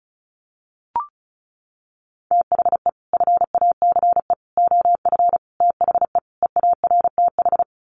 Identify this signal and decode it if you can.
Morse code, heard as a keyed tone, about 34 wpm, 700 Hz. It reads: THE FACE OF THE EARTH
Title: Sets of 5 Words Encore 35wpm